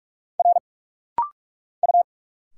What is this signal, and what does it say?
Morse code, heard as a keyed tone, about 46 wpm, 700 Hz. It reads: R U
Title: Single Letters - Rapid-Fire 45wpm